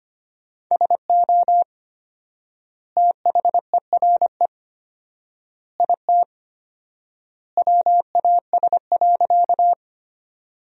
Morse code, heard as a keyed tone, about 25 wpm, 700 Hz. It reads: SO THERE IT WAS.
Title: 4x Word-Spacing - Sentences from Top 100 Words - 25wpm